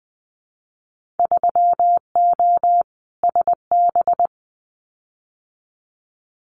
Morse code, heard as a keyed tone, about 20 wpm, 700 Hz. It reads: 3O SB
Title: Winter Field Day Exchanges Encore 20wpm